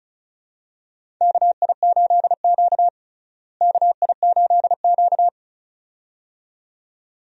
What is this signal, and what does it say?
Morse code, heard as a keyed tone, about 36 wpm, 700 Hz. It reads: KI8Q KI8Q
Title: USA Call Signs - 4 Characters - Repeated 2x - 35wpm